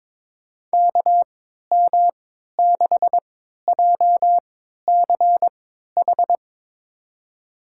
Morse code, heard as a keyed tone, about 22 wpm, spaced as for 13 wpm, 700 Hz. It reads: KM6JCH
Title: USA Call Signs - 6 Characters - 3x Character Spacing - 22wpm